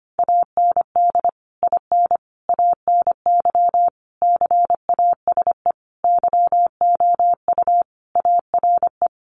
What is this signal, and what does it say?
Morse code, heard as a keyed tone, about 25 wpm, 700 Hz. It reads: AND IN ANY CASE YOU ARE
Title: Sentences from Top 1200 Words 25wpm